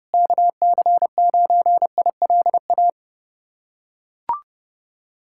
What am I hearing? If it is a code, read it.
Morse code, heard as a keyed tone, about 30 wpm, 700 Hz. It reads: KC9ILA
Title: USA Callsign DE Callsign 30wpm